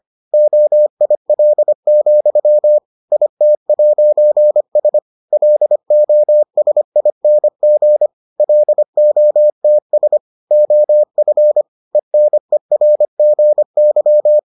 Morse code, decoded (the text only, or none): OIL, IT'S LOSING LOTS OF ENERGY